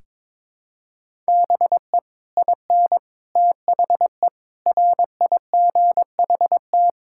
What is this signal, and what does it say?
Morse code, heard as a keyed tone, about 22 wpm, 700 Hz. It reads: BE IN THE RIGHT
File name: Sets of 9 Words - 22wpm